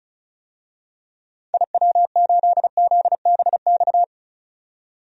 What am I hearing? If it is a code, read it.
Morse code, heard as a keyed tone, about 34 wpm, 700 Hz. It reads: IW8ZBX
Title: DXCC-100 Call Signs - 6 Characters - 35wpm